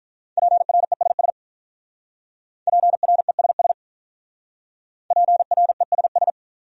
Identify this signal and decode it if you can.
Morse code, heard as a keyed tone, about 45 wpm, 700 Hz. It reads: PRESS PRESS PRESS